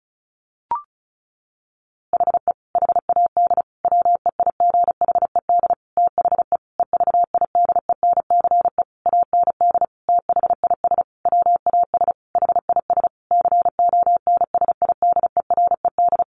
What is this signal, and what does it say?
Morse code, heard as a keyed tone, about 34 wpm, 700 Hz. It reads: HE HAD WEIGHED THE EVIDENCE AND THIS WAS HIS CONSIDERED
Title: A Sound Of Thunder 35wpm